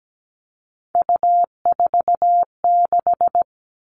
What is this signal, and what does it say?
Morse code, heard as a keyed tone, about 17 wpm, 700 Hz. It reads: U46
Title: Three Challenging Letter-Number Combinations 17wpm